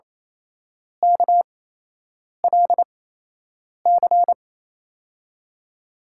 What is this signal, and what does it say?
Morse code, heard as a keyed tone, about 28 wpm, spaced as for 8 wpm, 700 Hz. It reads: KLC
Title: USA Call Signs - 6 Characters - 8x Character Spacing - 28wpm